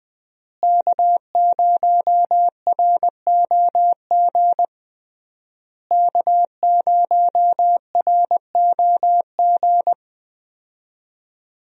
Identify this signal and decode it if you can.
Morse code, heard as a keyed tone, about 19 wpm, 700 Hz. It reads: K0ROG K0ROG
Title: USA Call Signs - 5 Characters - Repeated 3x - 20wpm